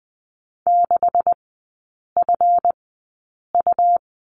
Morse code, heard as a keyed tone, about 20 wpm, spaced as for 8 wpm, 700 Hz. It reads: 6FU
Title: 159 Lesson 26e - Letter g - Three Characters 20wpm